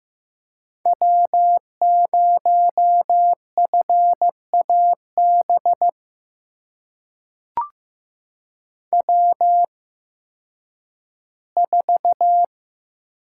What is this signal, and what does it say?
Morse code, heard as a keyed tone, about 15 wpm, spaced as for 10 wpm, 700 Hz. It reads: W0FAB W4
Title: USA Call Signs - 5 Characters - 8x Character Spacing - 15wpm